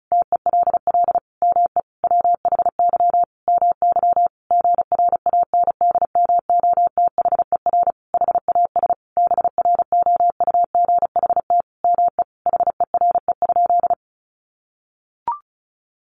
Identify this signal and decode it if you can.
Morse code, heard as a keyed tone, about 34 wpm, 700 Hz. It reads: TELL ME WHY MY GRANDMOTHER HAS BROUGHT ME HERE?
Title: Sentences from Top 1200 Words 35wpm